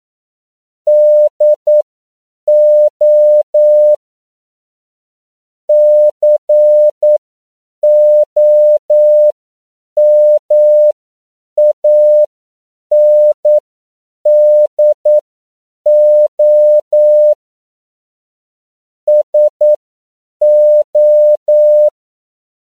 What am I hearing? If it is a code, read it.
Morse code, heard as a keyed tone, about 9 wpm, spaced as for 7 wpm, 600 Hz. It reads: DO COMANDO SO
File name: CW / 10PPM